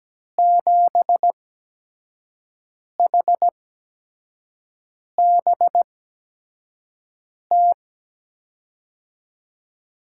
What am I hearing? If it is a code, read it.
Morse code, heard as a keyed tone, about 17 wpm, spaced as for 5 wpm, 700 Hz. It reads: 7HBT
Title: USA Call Signs - 6 Characters - 8x Character Spacing - 17wpm